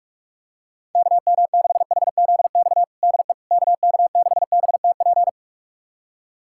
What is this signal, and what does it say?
Morse code, heard as a keyed tone, about 46 wpm, 700 Hz. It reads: KM6HZX DE KK6BTP
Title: USA Callsign DE Callsign 45wpm